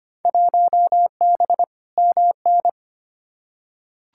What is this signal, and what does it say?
Morse code, heard as a keyed tone, about 25 wpm, 700 Hz. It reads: 1B MN
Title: Field Day Exchanges - 25wpm